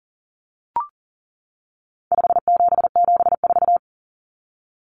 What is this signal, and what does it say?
Morse code, heard as a keyed tone, about 40 wpm, 700 Hz. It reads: 5774
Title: Common Cut Numbers - 4-Digit 40wpm